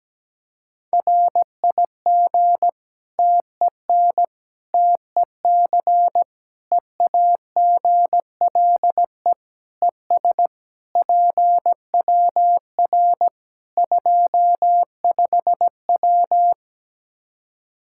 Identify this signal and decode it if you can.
Morse code, heard as a keyed tone, about 17 wpm, 700 Hz. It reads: RIG TEN TEC EAGLE ES PWR 25W
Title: Practice QSO - Exchange 2 - Info - 17wpm